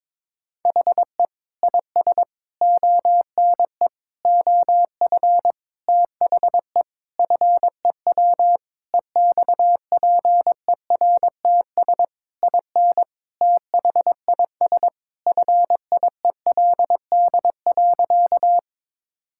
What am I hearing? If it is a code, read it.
Morse code, heard as a keyed tone, about 22 wpm, 700 Hz. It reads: HE IS ONE OF THE FEW EXPERTS IN THIS FIELD.
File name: practice / 22wpm / Sentences with 4-Word Phrases - 22wpm